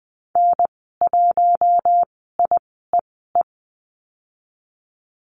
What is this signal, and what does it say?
Morse code, heard as a keyed tone, about 20 wpm, spaced as for 14 wpm, 700 Hz. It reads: N1IEE
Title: 069 Lesson 12g - Letter d - Call Signs 20wpm